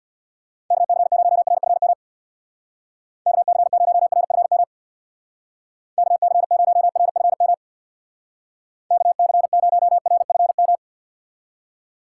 Morse code, heard as a keyed tone, about 63 wpm, 700 Hz. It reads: KC0RFM KC0RFM KC0RFM KC0RFM